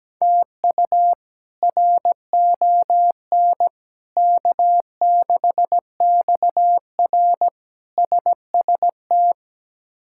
Morse code, heard as a keyed tone, about 17 wpm, 700 Hz. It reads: TU RON K6XR SST
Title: K1USN SST QSO Practice - 17wpm